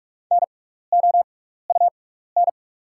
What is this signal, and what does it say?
Morse code, heard as a keyed tone, about 43 wpm, spaced as for 15 wpm, 690 Hz. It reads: NOUN
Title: 6x Character Spacing - Top 500 Words - 45wpm